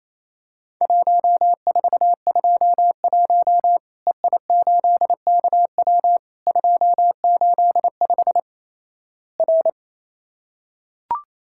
Morse code, heard as a keyed tone, about 28 wpm, 700 Hz. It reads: 1421 EI8KW 285 R
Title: Worked All Europe DX Contest - QTC Exchange - 28wpm